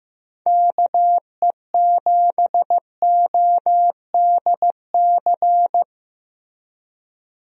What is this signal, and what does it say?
Morse code, heard as a keyed tone, about 15 wpm, 700 Hz. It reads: KE7ODC
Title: USA Call Signs - 6 Characters - Repeated 3x - 15wpm